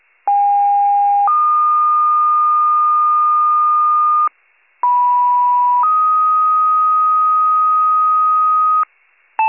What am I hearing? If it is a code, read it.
Two-tone page, heard as a one-second tone, then a three-second tone, about 810 then 1220 Hz, about 980 then 1250 Hz. It is heard from 0.3 to 8.8 s.